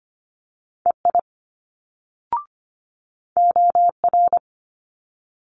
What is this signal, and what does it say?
Morse code, heard as a keyed tone, about 25 wpm, 700 Hz. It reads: EI OR
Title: Multi-letter Phonemes Quick Fire 25wpm